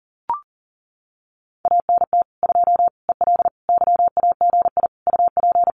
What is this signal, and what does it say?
Morse code, heard as a keyed tone, about 37 wpm, 700 Hz. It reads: ANT 2 EL YAGI UP
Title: Common QSO Phrases Encore 40wpm